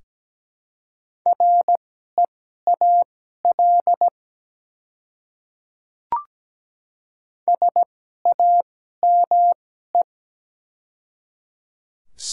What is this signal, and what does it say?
Morse code, heard as a keyed tone, about 17 wpm, spaced as for 12 wpm, 700 Hz. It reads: REAL SAME
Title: 2x Character Spacing - Top 300 Words - 17wpm